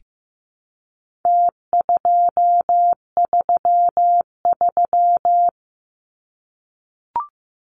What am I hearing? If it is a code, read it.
Morse code, heard as a keyed tone, about 15 wpm, 700 Hz. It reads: T233
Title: Numbers - Contest Style - 4-Digit 15wpm